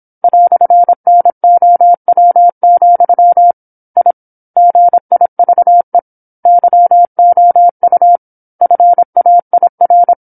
Morse code, heard as a keyed tone, about 27 wpm, 700 Hz. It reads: "NOW, I GIVE YOU FAIR
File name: 08-Chapter-08 26 WPM